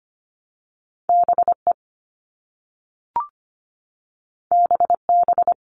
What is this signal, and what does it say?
Morse code, heard as a keyed tone, about 25 wpm, 700 Hz. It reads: BE BB
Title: All Cut Numbers - 2-Digit 25wpm